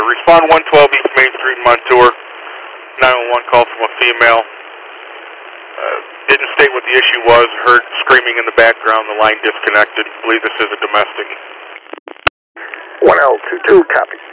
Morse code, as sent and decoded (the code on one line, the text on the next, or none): none